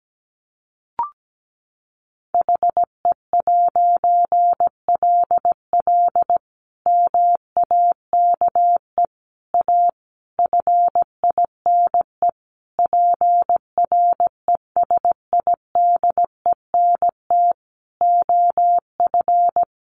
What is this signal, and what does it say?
Morse code, heard as a keyed tone, about 17 wpm, 700 Hz. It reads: HE'LL MAKE A FINE PRESIDENT OF
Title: A Sound Of Thunder 17wpm